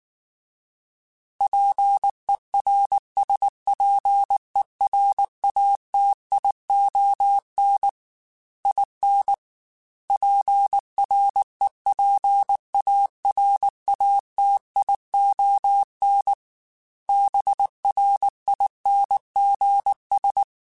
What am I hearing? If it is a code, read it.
Morse code, heard as a keyed tone, about 19 wpm, 800 Hz. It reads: PERSPERATION IN PREPARATION BRINGS
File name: Quotes / Quotes-mp3s / Persperation @18